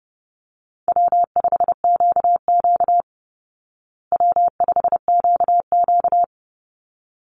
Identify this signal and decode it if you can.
Morse code, heard as a keyed tone, about 30 wpm, 700 Hz. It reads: W5QQ W5QQ
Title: Call Signs Repeated 3x - 30wpm